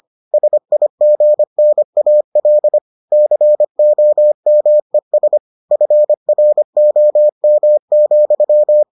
Morse code, heard as a keyed tone, about 25 wpm, 600 Hz. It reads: SIGNAL COMES FROM,